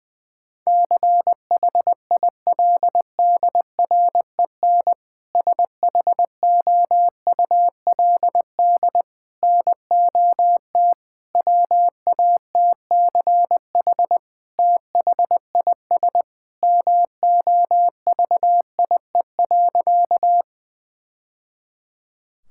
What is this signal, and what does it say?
Morse code, heard as a keyed tone, about 20 wpm, 700 Hz. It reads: CHILDREN SHOULD NOT WATCH THIS MOVIE.
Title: Sentences from Top 1000 Words - Review Part 5 - 20wpm